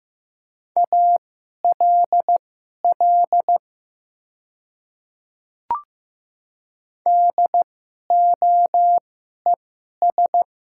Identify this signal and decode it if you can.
Morse code, heard as a keyed tone, about 15 wpm, spaced as for 11 wpm, 700 Hz. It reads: ALL DOES
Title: 2x Character Spacing - Top 300 Words - 15wpm